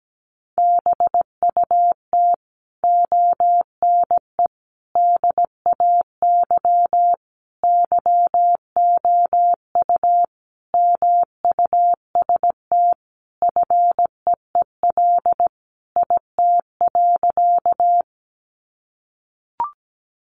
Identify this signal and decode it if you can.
Morse code, heard as a keyed tone, about 17 wpm, 700 Hz. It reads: BUT ONE DAY YOU MUST FEEL IT.
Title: Sentences from Top 400 Words 17wpm